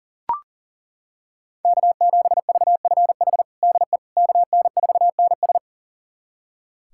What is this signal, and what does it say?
Morse code, heard as a keyed tone, about 40 wpm, 700 Hz. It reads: K7VFH DE KN4NS
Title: USA Callsign DE Callsign - 5 Characters - 40wpm